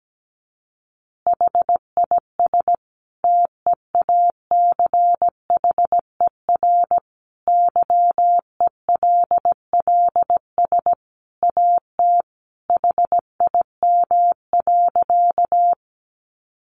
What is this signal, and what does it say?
Morse code, heard as a keyed tone, about 17 wpm, 700 Hz. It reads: HIS TEACHER YELLS AT HIM.
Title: Additional Easy Phrases 17wpm